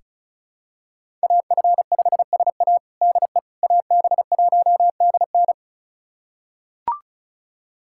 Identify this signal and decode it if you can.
Morse code, heard as a keyed tone, about 34 wpm, 700 Hz. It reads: AF5SA DE AB1DN
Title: USA Callsign DE Callsign - 5 Characters - 35wpm